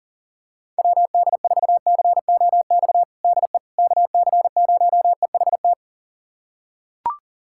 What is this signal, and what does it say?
Morse code, heard as a keyed tone, about 40 wpm, 700 Hz. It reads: WD4COX DE KC0EHT
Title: USA Callsign DE Callsign 40wpm